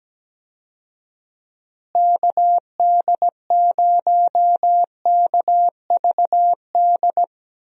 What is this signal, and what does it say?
Morse code, heard as a keyed tone, about 17 wpm, 700 Hz. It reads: KD0KVD
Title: USA Call Signs - 6 Characters - Repeated 2x - 17wpm